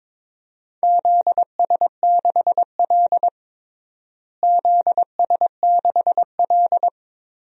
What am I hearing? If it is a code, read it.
Morse code, heard as a keyed tone, about 22 wpm, 700 Hz. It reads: ZS6L ZS6L
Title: DXCC-100 Call Signs - 4 Characters - Repeated 2x - 22wpm